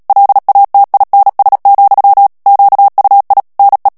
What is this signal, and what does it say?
Morse code, heard as a keyed tone, about 37 wpm, 800 Hz. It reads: LATINS, QUI NE
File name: CW / FICHIERS-MP3 / FRANCAIS TEXTE CLAIR 2 VITESSE 35 MOTS MINUTES